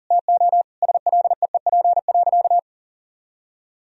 Morse code, heard as a keyed tone, about 37 wpm, 700 Hz. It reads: TO SLEEP.
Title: Sentences from Top 1000 Words - Review Part 2 - 40wpm